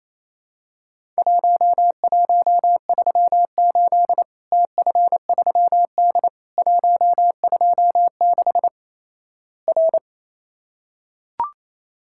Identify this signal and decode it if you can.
Morse code, heard as a keyed tone, about 28 wpm, 700 Hz. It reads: 1138 TF3D 126 R
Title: Worked All Europe DX Contest - QTC Exchange - 28wpm